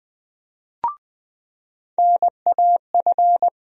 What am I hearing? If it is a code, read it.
Morse code, heard as a keyed tone, about 20 wpm, 700 Hz. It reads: NAF
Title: USA Call Sign Postfixes - 3 Letters 20wpm